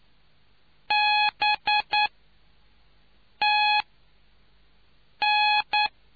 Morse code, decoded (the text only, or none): B T N